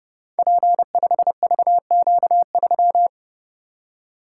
Morse code, heard as a keyed tone, about 30 wpm, 700 Hz. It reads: P5VQ3
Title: Code Groups - 5 Letter:Numbers - 30wpm